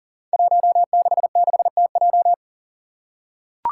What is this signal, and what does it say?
Morse code, heard as a keyed tone, about 40 wpm, 700 Hz. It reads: 166TJ
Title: Five Challenging Letter-Number Combinations 40wpm